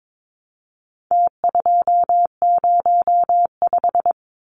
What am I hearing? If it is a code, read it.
Morse code, heard as a keyed tone, about 22 wpm, 700 Hz. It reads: T205
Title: Numbers - Contest Style - 4-Digit 22wpm